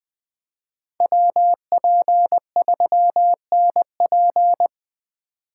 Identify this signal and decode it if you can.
Morse code, heard as a keyed tone, about 20 wpm, 700 Hz. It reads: WP3NP
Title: Speed Racing - USA 2x2 Call Signs 17wpm